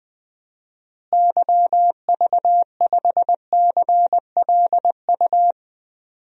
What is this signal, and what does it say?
Morse code, heard as a keyed tone, about 20 wpm, 700 Hz. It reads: YV5CLU
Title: DXCC-100 Call Signs - 6 Characters - Repeated 2x - 20wpm